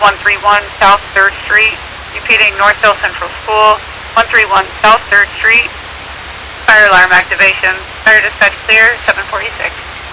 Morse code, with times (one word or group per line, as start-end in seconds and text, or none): none